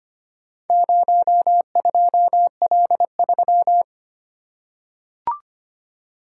Morse code, heard as a keyed tone, about 25 wpm, 700 Hz. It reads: 02L3
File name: practice / 25wpm / Four Letter-Number Combinations - 25wpm